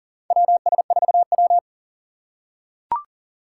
Morse code, heard as a keyed tone, about 40 wpm, 700 Hz. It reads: WS4W
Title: USA Call Signs - 6x Character Spacing - 40wpm